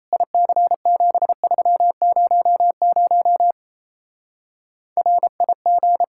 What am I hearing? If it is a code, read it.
Morse code, heard as a keyed tone, about 33 wpm, 700 Hz. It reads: IC7300 RIG
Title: Speed-Racing Common QSO Phrases Encore - 28wpm